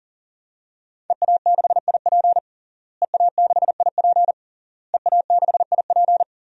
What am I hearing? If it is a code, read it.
Morse code, heard as a keyed tone, about 37 wpm, 700 Hz. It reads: EA6IP EA6IP EA6IP
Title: DXCC-100 Call Signs - 5 Characters - Repeated 3x - 40wpm